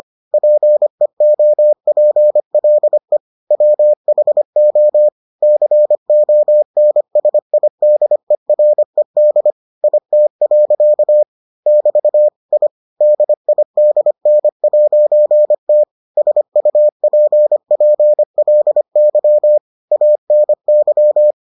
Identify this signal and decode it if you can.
Morse code, heard as a keyed tone, about 25 wpm, 600 Hz. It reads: PEOPLE WHO CONSIDERED IT. <BT> I DIDN'T SUPPLY ANY